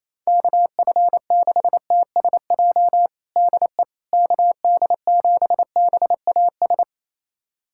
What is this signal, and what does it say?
Morse code, heard as a keyed tone, about 27 wpm, 700 Hz. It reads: KF6TSJ DE KD7BAS
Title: USA Callsign DE Callsign - 6 Characters - 28wpm